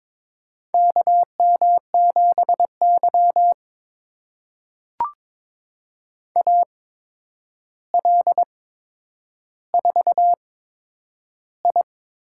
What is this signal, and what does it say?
Morse code, heard as a keyed tone, about 22 wpm, spaced as for 9 wpm, 700 Hz. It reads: KM7Y AL4I
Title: USA Call Signs - 4 Characters - 8x Character Spacing - 22wpm